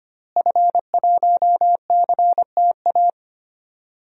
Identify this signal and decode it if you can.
Morse code, heard as a keyed tone, about 25 wpm, 700 Hz. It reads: F1CTA
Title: DXCC-100 Call Signs - 5 Characters - Repeated 2x - 25wpm